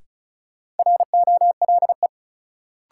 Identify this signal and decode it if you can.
Morse code, heard as a keyed tone, about 34 wpm, 700 Hz. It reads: ROLE
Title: Sentences from Top 1500 Words - New Word Drill Alt - 35wpm